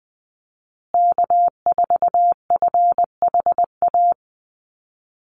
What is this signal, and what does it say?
Morse code, heard as a keyed tone, about 20 wpm, 700 Hz. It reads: K4FHA
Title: USA Call Signs Repeated 2x 20wpm